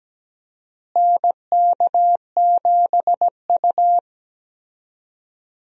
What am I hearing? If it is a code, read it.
Morse code, heard as a keyed tone, about 17 wpm, 700 Hz. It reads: NK7U
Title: USA Call Signs - 4 Characters - Repeated 3x - 17wpm